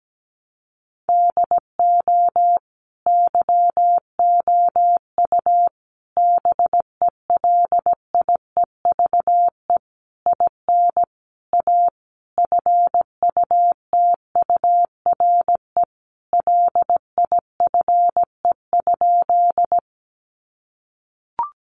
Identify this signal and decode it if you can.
Morse code, heard as a keyed tone, about 17 wpm, 700 Hz. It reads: DO YOU BELIEVE IN A FUTURE LIFE?
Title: Sentences from Top 1000 Words 17wpm